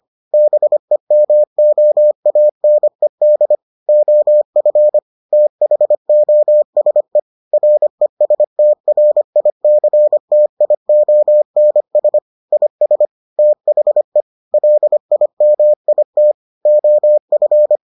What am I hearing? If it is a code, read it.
Morse code, heard as a keyed tone, about 25 wpm, 600 Hz. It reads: BEMOANED OF THOSE RESTRICTIONS IS THE LIMIT OF